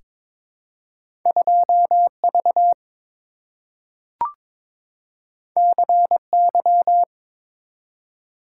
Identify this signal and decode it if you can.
Morse code, heard as a keyed tone, about 22 wpm, 700 Hz. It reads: 2V CY
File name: practice / 22wpm / Two Letter-Number Combinations - 22wpm